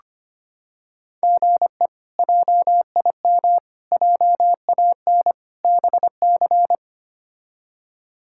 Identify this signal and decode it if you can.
Morse code, heard as a keyed tone, about 25 wpm, 700 Hz. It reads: GE JIM JAN BC
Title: K1USN SST QSO Practice - 25wpm